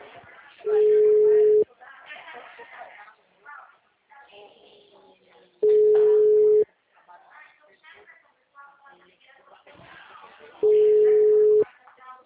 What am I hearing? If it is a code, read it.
Ringback tone (Europe, 425 Hz), with speech.
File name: out- -2011- - - .42849